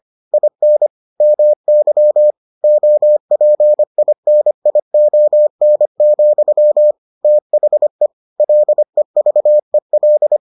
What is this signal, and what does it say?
Morse code, heard as a keyed tone, about 25 wpm, 600 Hz. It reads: IN MY OPINION, THE LEVEL